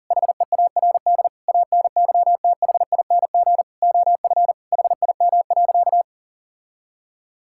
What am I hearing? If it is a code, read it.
Morse code, heard as a keyed tone, about 40 wpm, 700 Hz. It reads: HEARD ANYTHING OF HIM.
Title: Sentences from Top 1600 Words 40wpm